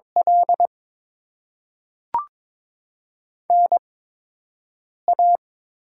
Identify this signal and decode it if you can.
Morse code, heard as a keyed tone, about 21 wpm, 700 Hz. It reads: L N A
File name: Top 20 QSO Elements - 8x Character Spacing - 22wpm